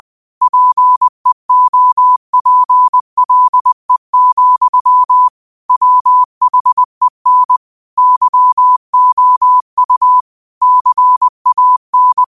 Morse code, heard as a keyed tone, about 20 wpm, 1000 Hz. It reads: PEOPLE, WHEN YOU CAN